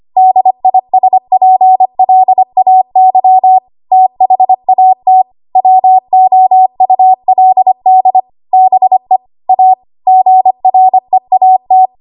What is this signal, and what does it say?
Morse code, heard as a keyed tone, about 25 wpm, 750 Hz. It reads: DISPLAY THAT WOULD BE A GREAT
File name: Archive / 25 WPM / 25WPM